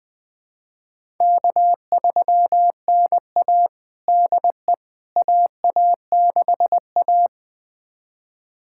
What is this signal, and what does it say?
Morse code, heard as a keyed tone, about 20 wpm, 700 Hz. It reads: K3NA DE AA6A
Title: USA Callsign DE Callsign - 4 Characters - 20wpm